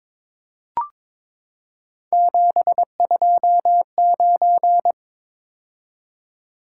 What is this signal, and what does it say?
Morse code, heard as a keyed tone, about 22 wpm, 700 Hz. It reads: 729
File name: practice / 22wpm / Three Letter-Number Combinations - 22wpm